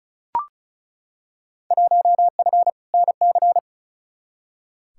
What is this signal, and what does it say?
Morse code, heard as a keyed tone, about 36 wpm, 700 Hz. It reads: 1F NC
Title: Field Day Exchanges - 35wpm